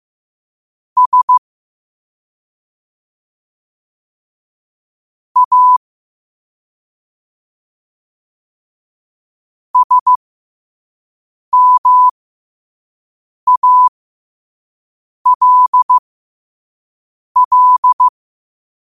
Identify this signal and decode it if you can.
Morse code, heard as a keyed tone, about 15 wpm, spaced as for 5 wpm, 1000 Hz. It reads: S A SMALL
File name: qotdfiles / 5wpm